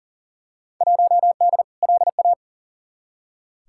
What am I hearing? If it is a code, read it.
Morse code, heard as a keyed tone, about 40 wpm, 700 Hz. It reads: 1D LA
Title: Field Day Exchanges - 40wpm